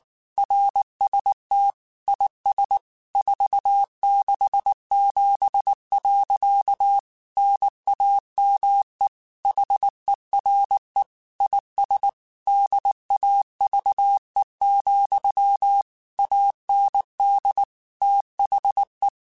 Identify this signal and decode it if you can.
Morse code, heard as a keyed tone, about 19 wpm, 800 Hz. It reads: RST IS 467. NAME HERE IS DAVE, AND THE
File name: QSOs / QSOs-mp3s / WA7QOX 2@18